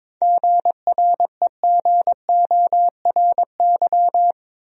Morse code, heard as a keyed tone, about 21 wpm, 700 Hz. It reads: GREGORY
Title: USA Common Names - 22wpm